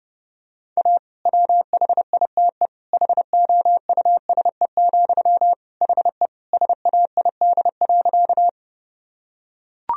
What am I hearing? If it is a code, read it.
Morse code, heard as a keyed tone, about 30 wpm, 700 Hz. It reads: A WHITE HOUSE, HE SAID.
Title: Sentences from Top 1000 Words - Review Part 1 - 30wpm